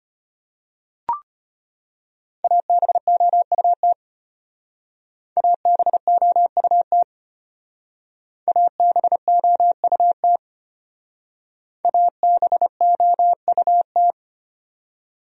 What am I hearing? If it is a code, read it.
Morse code, heard as a keyed tone, about 31 wpm, 700 Hz. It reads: ABOUT ABOUT ABOUT ABOUT
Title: Speed-Racing Top 100 Words 25wpm